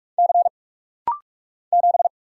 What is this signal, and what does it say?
Morse code, heard as a keyed tone, about 44 wpm, 700 Hz. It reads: C 7
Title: Single Letter-Number Alt - Rapid-Fire 45wpm